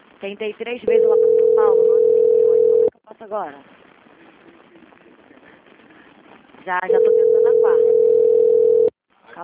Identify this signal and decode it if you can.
Ringback tone (North America), 440 + 480 Hz, with speech.
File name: out- -2027- - - .13917